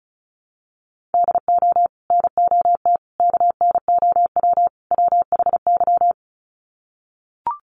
Morse code, heard as a keyed tone, about 34 wpm, 700 Hz. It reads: DO NOT KNOW WHY
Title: Sets of 4 Words 35wpm